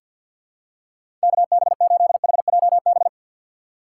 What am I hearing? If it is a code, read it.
Morse code, heard as a keyed tone, about 53 wpm, 700 Hz. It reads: KB8HJB
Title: USA Call Signs - 6 Characters - 4x Character Spacing - 50wpm